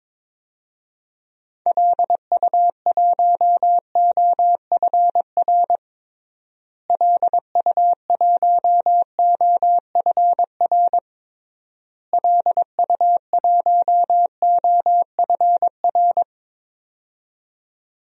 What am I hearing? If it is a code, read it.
Morse code, heard as a keyed tone, about 22 wpm, 700 Hz. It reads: LU1OFR LU1OFR LU1OFR
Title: DXCC-100 Call Signs - 6 Characters - Repeated 3x - 22wpm